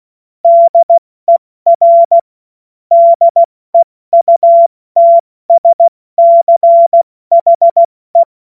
Morse code, heard as a keyed tone, about 16 wpm, 680 Hz. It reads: DER DEUTSCHE
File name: Morsekurs / Witze / Witz 30. Dezember 2025 70 F80